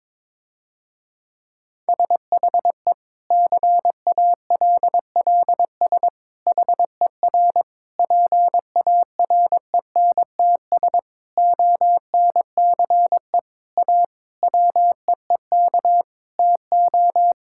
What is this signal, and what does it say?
Morse code, heard as a keyed tone, about 22 wpm, 700 Hz. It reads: SHE CALLS HER PARENTS ONCE A WEEK TO
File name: Difficult Sentences with Hints - 22wpm